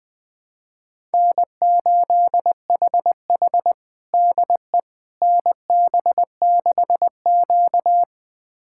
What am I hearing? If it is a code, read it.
Morse code, heard as a keyed tone, about 21 wpm, 700 Hz. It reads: N8HH DE NB6Q
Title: USA Callsign DE Callsign - 4 Characters - 20wpm